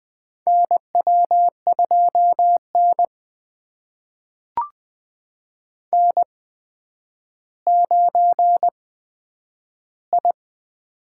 Morse code, heard as a keyed tone, about 20 wpm, spaced as for 10 wpm, 700 Hz. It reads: NW2N N9I
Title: USA Call Signs - 8x Character Spacing - 20wpm